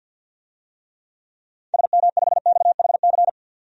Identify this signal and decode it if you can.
Morse code, heard as a keyed tone, about 50 wpm, 700 Hz. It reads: SM5XHC
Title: DXCC-100 Call Signs - 6 Characters - 50wpm